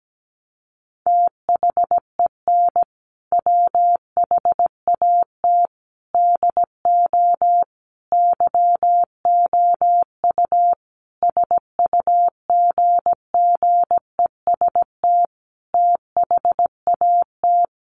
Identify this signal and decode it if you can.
Morse code, heard as a keyed tone, about 17 wpm, 700 Hz. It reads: THEN WHAT DO YOU SUGGEST THAT
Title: Sentences from Top 1300 Words 17wpm